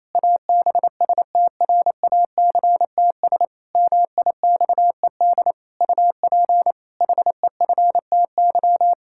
Morse code, heard as a keyed tone, about 28 wpm, 700 Hz. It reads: ABSTRACTS MIXED UP HEFTY
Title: Pangram Challenge v2 - 28wpm